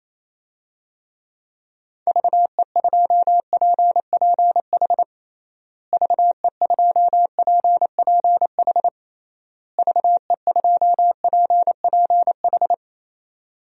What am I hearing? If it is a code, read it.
Morse code, heard as a keyed tone, about 28 wpm, 700 Hz. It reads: VE2PPH VE2PPH VE2PPH
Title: DXCC-100 Call Signs - 6 Characters - Repeated 3x - 28wpm